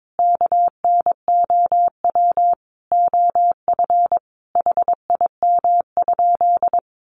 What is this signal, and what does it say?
Morse code, heard as a keyed tone, about 22 wpm, 700 Hz. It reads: KNOW OF HIM?
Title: Sentences from Top 100 Words 22wpm